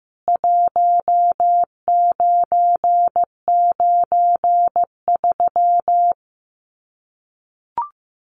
Morse code, heard as a keyed tone, about 15 wpm, 700 Hz. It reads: 1993
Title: CWT Exchange 15wpm